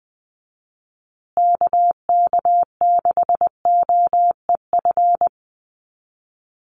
Morse code, heard as a keyed tone, about 20 wpm, 700 Hz. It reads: KK6OEF
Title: USA Call Signs Repeated 3x 20wpm